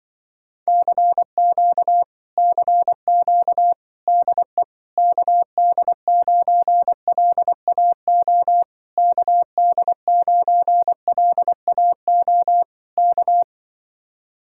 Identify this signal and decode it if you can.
Morse code, heard as a keyed tone, about 24 wpm, 700 Hz. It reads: CQ CQ DE KD9LAO KD9LAO K
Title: Speed-Racing Common QSO Phrases - 20wpm